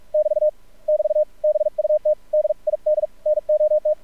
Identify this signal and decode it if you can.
Morse code, heard as a keyed tone, about 44 wpm, 610 Hz. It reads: <BT> <BT> BUT DID NOT
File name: archived / CRSnet